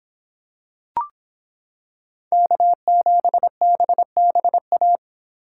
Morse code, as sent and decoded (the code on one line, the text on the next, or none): -.- --... -... -... .-
K7BBA